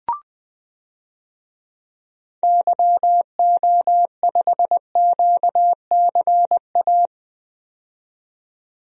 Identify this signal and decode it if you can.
Morse code, heard as a keyed tone, about 20 wpm, 700 Hz. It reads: YO5QCA
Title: DXCC-100 Call Signs - 6 Characters - 20wpm